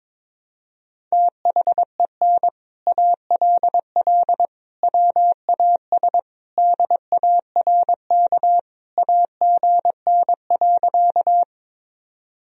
Morse code, heard as a keyed tone, about 22 wpm, 700 Hz. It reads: THEN ALL WAS DARK AGN.
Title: Sentences from Top 500 Words and Abbreviations 22wpm